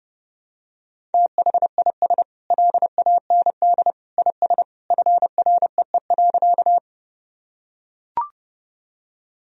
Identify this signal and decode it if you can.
Morse code, heard as a keyed tone, about 30 wpm, 700 Hz. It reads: THIS LAND IS FREE.
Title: Sentences from Top 1000 Words - Review Part 1 - 30wpm